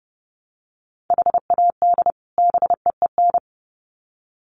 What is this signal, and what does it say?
Morse code, heard as a keyed tone, about 30 wpm, 700 Hz. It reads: HAD BEEN
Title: Sets of 2 Words 30wpm